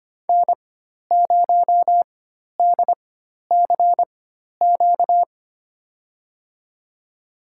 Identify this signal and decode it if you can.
Morse code, heard as a keyed tone, about 25 wpm, spaced as for 12 wpm, 700 Hz. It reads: N0DCQ
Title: USA Call Signs - 5 Characters - 4x Character Spacing - 25wpm